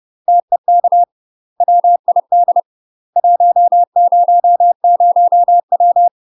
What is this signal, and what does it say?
Morse code, heard as a keyed tone, about 30 wpm, 700 Hz. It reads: TEK WID 100W
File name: UK MASTER QSOs TRAINING / 30WPM / UK MASTER QSOs 0046